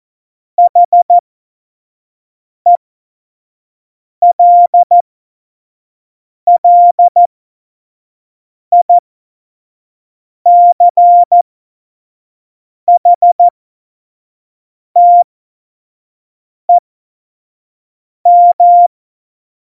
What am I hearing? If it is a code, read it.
Morse code, heard as a keyed tone, about 14 wpm, spaced as for 5 wpm, 700 Hz. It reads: HELLICHTEM
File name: Morsekurs / Witze / Witz 28. Januar 2020, 25 F70